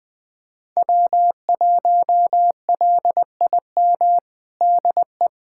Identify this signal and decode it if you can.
Morse code, heard as a keyed tone, about 20 wpm, 700 Hz. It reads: W1LIM DE
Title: USA Callsign DE Callsign - 5 Characters - 20wpm